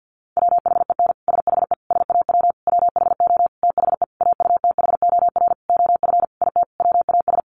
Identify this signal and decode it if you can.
Morse code, heard as a keyed tone, about 50 wpm, 700 Hz. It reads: WHEN SHE SAW WHO THE AUTHOR OF IT WAS
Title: Sentences from Top 1200 Words 50wpm